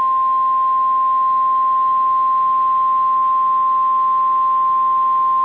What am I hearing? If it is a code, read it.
Emergency Alert System (SAME) tones (the 1050 Hz attention tone). It is heard from 0.9 to 5.2 s.